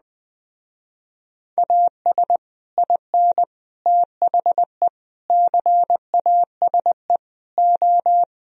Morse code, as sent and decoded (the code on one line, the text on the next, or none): .- ... / .. -. / - .... / -.-. .- ... / ---
AS IN THE CASE O